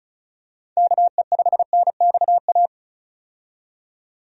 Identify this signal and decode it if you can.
Morse code, heard as a keyed tone, about 34 wpm, 700 Hz. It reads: KE5NXA
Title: USA Call Signs - 6 Characters - 35wpm